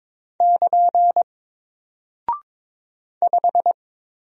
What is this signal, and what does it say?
Morse code, heard as a keyed tone, about 22 wpm, 700 Hz. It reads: <KN> 5
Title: Single Character (Letter:Number:Punctuation:Prosign) 22wpm